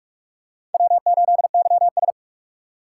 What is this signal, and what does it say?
Morse code, heard as a keyed tone, about 44 wpm, 700 Hz. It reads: W8YS
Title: USA Call Signs - 4 Characters - 2x Character Spacing - 45wpm